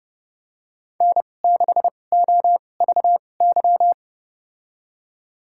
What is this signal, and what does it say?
Morse code, heard as a keyed tone, about 30 wpm, spaced as for 22 wpm, 700 Hz. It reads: N6OVY
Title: USA Call Signs - 5 Characters - 2x Character Spacing - 30wpm